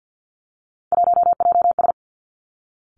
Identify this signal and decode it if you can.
Morse code, heard as a keyed tone, about 50 wpm, 700 Hz. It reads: 1JS
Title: Three Challenging Letter-Number Combinations 50wpm